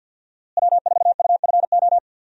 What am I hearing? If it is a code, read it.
Morse code, heard as a keyed tone, about 50 wpm, 700 Hz. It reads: W4UFO
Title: USA Call Signs - 5 Characters - Repeated 2x - 50wpm